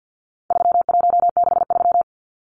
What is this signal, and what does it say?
Morse code, heard as a keyed tone, about 50 wpm, 700 Hz. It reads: 3163
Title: Numbers - 4-Digit 50wpm